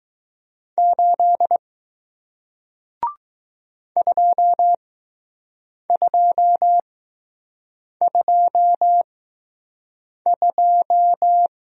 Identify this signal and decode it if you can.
Morse code, heard as a keyed tone, about 20 wpm, spaced as for 6 wpm, 700 Hz. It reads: 8 2222